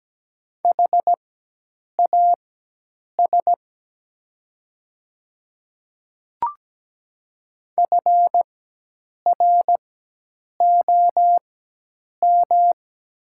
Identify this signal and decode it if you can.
Morse code, heard as a keyed tone, about 17 wpm, spaced as for 8 wpm, 700 Hz. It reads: HAS FROM
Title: 4x Character Spacing - Top 100 Words - 17wpm